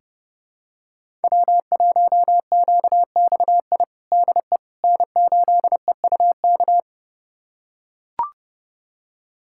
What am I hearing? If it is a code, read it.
Morse code, heard as a keyed tone, about 30 wpm, 700 Hz. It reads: W1QXI DE N8EUK
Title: USA Callsign DE Callsign - 5 Characters - 30wpm